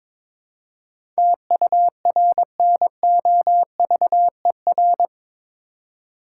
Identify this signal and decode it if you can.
Morse code, heard as a keyed tone, about 22 wpm, 700 Hz. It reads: TURNOVER
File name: practice / 22wpm / Eight Letter Words - 22wpm